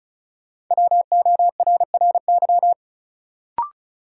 Morse code, heard as a keyed tone, about 34 wpm, 700 Hz. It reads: WORRY
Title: Sentences from Top 1100 Words - New Word Drill - 35wpm